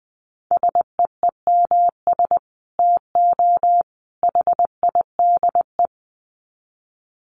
Morse code, heard as a keyed tone, about 20 wpm, 700 Hz. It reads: SEEMS TO HIDE
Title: Dit Challenge 20wpm